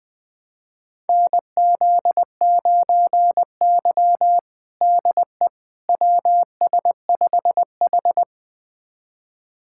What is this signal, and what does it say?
Morse code, heard as a keyed tone, about 19 wpm, 700 Hz. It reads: NZ9Y DE WS5H
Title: USA Callsign DE Callsign - 4 Characters - 20wpm